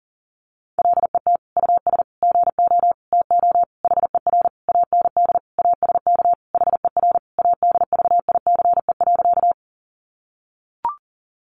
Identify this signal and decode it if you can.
Morse code, heard as a keyed tone, about 40 wpm, 700 Hz. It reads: LET US GO TO HER AND ASK HER ADVICE.
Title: Sentences from Top 1100 Words 40wpm